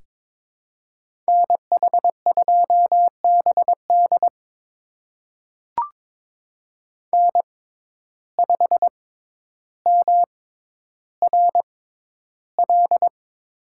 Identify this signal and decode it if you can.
Morse code, heard as a keyed tone, about 22 wpm, spaced as for 11 wpm, 700 Hz. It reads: NH2BD N5MRL
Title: USA Call Signs - 5 Characters - 6x Character Spacing - 22wpm